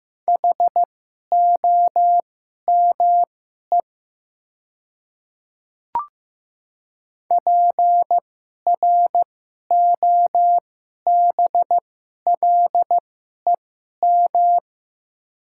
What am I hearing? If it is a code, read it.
Morse code, heard as a keyed tone, about 15 wpm, spaced as for 11 wpm, 700 Hz. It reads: HOME PROBLEM
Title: 2x Character Spacing - Top 500 Words - 15wpm